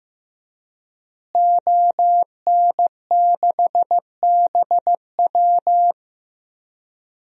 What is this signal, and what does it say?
Morse code, heard as a keyed tone, about 15 wpm, 700 Hz. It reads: ON6BW
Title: DXCC-100 Call Signs - 5 Characters - Repeated 2x - 15wpm